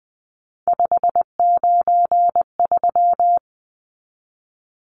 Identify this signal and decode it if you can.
Morse code, heard as a keyed tone, about 20 wpm, 700 Hz. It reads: 593
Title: Numbers - 3-Digit 20wpm